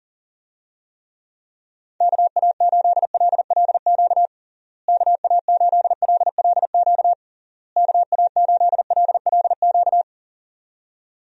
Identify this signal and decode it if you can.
Morse code, heard as a keyed tone, about 40 wpm, 700 Hz. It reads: KA8LLQ KA8LLQ KA8LLQ
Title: USA Call Signs - 6 Characters - Repeated 3x - 40wpm